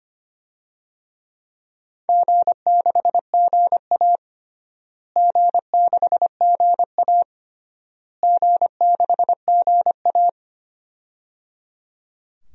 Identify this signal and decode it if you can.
Morse code, heard as a keyed tone, about 25 wpm, 700 Hz. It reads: G6GA G6GA G6GA
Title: DXCC-100 Call Signs - 4 Characters - Repeated 3x - 25wpm